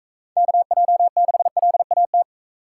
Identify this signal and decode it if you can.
Morse code, heard as a keyed tone, about 40 wpm, 700 Hz. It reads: KJ6LAT